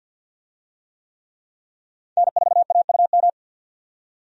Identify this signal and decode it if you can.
Morse code, heard as a keyed tone, about 52 wpm, 700 Hz. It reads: N4AUM
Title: USA Call Signs - 5 Characters - 50wpm